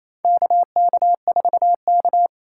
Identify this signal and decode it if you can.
Morse code, heard as a keyed tone, about 28 wpm, 700 Hz. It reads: KK4K
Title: USA Call Signs - 4 Characters - 4x Character Spacing - 28wpm